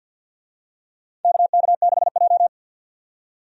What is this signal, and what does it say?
Morse code, heard as a keyed tone, about 47 wpm, 700 Hz. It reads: KK6J
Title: USA Call Signs - 4 Characters - 50wpm